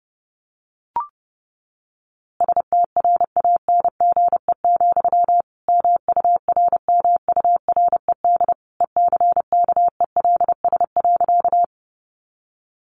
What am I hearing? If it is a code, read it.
Morse code, heard as a keyed tone, about 30 wpm, 700 Hz. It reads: STRANGE, MURMURED ECKELS.
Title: A Sound Of Thunder 30wpm